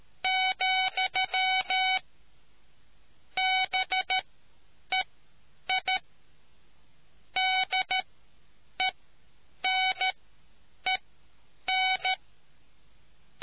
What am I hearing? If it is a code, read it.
Morse code, heard as a keyed tone, about 13 wpm, spaced as for 8 wpm, 750 Hz. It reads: , BEI DENEN